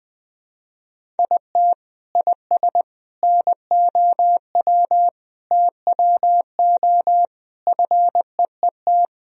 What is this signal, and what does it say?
Morse code, heard as a keyed tone, about 21 wpm, 700 Hz. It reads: IT IS NOW TWO FEET
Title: Sentences from Top 1000 Words - Review Part 1 - 20wpm